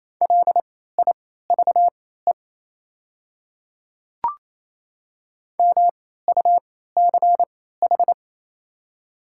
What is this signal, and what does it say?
Morse code, heard as a keyed tone, about 28 wpm, spaced as for 16 wpm, 700 Hz. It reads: LIVE MUCH
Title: 3x Character Spacing - Top 200 Words - 28wpm